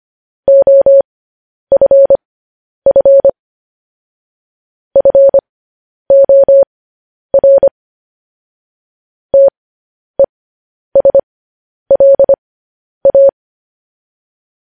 Morse code, heard as a keyed tone, about 25 wpm, spaced as for 10 wpm, 550 Hz. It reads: OFF FOR TESLA
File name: podcast / News headlines-1x-25x10